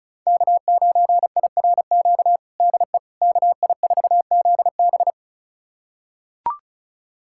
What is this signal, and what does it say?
Morse code, heard as a keyed tone, about 36 wpm, 700 Hz. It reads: K9IRQ DE KI4ZB
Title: USA Callsign DE Callsign - 5 Characters - 35wpm